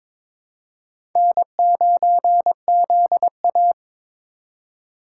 Morse code, heard as a keyed tone, about 22 wpm, 700 Hz. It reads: N9ZA
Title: USA Call Signs - 3x Character Spacing - 22wpm